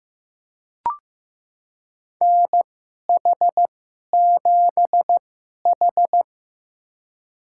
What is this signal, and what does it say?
Morse code, heard as a keyed tone, about 15 wpm, spaced as for 11 wpm, 700 Hz. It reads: NH7H
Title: USA Call Signs - 2x Character Spacing - 15wpm